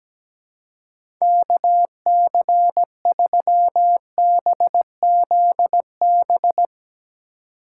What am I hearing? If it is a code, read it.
Morse code, heard as a keyed tone, about 17 wpm, 700 Hz. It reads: KC3BZB
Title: USA Call Signs - 6 Characters - 3x Character Spacing - 17wpm